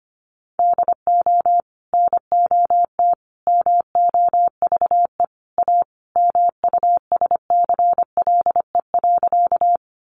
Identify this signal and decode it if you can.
Morse code, heard as a keyed tone, about 25 wpm, 700 Hz. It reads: DO NOT MOVE A MUSCLE.
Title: Sentences from Top 1300 Words 25wpm